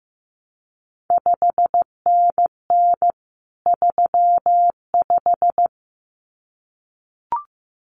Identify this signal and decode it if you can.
Morse code, heard as a keyed tone, about 15 wpm, 700 Hz. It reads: 5NN 35
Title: CQ WW DX Contest Exchange - 15wpm